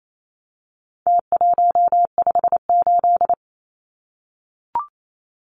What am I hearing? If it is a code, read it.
Morse code, heard as a keyed tone, about 28 wpm, 700 Hz. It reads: T158
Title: Numbers - Contest Style - 4-Digit 28wpm